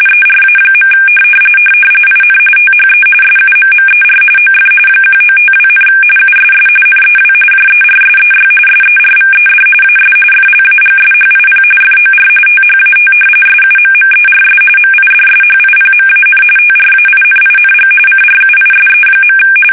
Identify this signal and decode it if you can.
RTTY, about 45.45 baud. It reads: ERA. IN THE SUMMER OF 1952, USS NEW JERSEY SERVED AS FLAGSHIP FOR THAT SUMMERS MIDSHIPMAN TRAINING CRUISE.
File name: fsk850 / battleshipnj